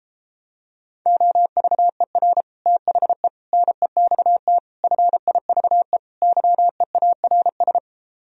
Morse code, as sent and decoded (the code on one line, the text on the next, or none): --- ...- . .-. / - .... / -. -..- - / ..-. .. ...- . / -.-- . .- .-. ...
OVER THE NEXT FIVE YEARS